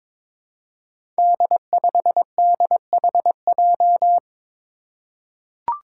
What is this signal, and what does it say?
Morse code, heard as a keyed tone, about 23 wpm, 700 Hz. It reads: D5DHJ
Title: Five Challenging Letter-Number Combinations 22wpm